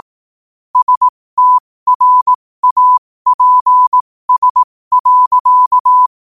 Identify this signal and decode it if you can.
Morse code, heard as a keyed tone, about 18 wpm, spaced as for 15 wpm, 1000 Hz. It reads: STRAPS.
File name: qotdfiles / 15wpm